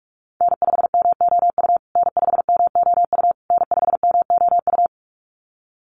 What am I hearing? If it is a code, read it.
Morse code, heard as a keyed tone, about 44 wpm, 700 Hz. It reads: N5MOU N5MOU N5MOU
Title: USA Call Signs Repeated 3x 45wpm